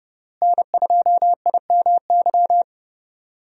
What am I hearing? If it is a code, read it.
Morse code, heard as a keyed tone, about 29 wpm, 700 Hz. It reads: N2IMY
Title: USA Call Signs - 5 Characters - Repeated 2x - 30wpm